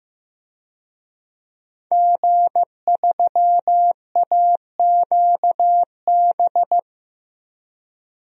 Morse code, heard as a keyed tone, about 15 wpm, 700 Hz. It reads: G3AQB
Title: DXCC-100 Call Signs - 5 Characters - Repeated 2x - 15wpm